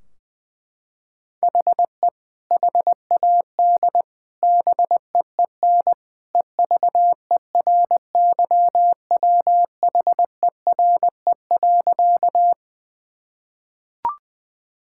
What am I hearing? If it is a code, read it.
Morse code, heard as a keyed tone, about 20 wpm, 700 Hz. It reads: HE HAD BEEN EVERYWHERE.
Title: Sentences from Top 1600 Words 20wpm